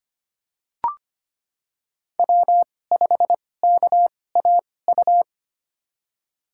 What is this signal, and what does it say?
Morse code, heard as a keyed tone, about 25 wpm, spaced as for 18 wpm, 700 Hz. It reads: W5KAU
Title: USA Call Signs - 2x Character Spacing - 25wpm